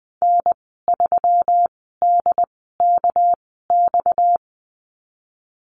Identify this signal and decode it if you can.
Morse code, heard as a keyed tone, about 20 wpm, spaced as for 15 wpm, 700 Hz. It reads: N3DKX
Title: 231 Lesson 37g - Letter x - Call Signs 20wpm